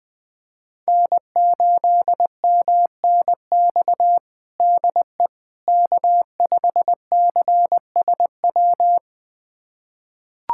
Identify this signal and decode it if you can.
Morse code, heard as a keyed tone, about 21 wpm, 700 Hz. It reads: N8MNX DE K5CSW
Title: USA Callsign DE Callsign 20wpm